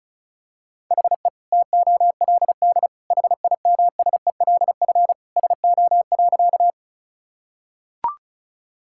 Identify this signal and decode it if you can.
Morse code, heard as a keyed tone, about 36 wpm, 690 Hz. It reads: HE TOLD HIMSELF SO.
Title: Sentences from Top 1600 Words 35wpm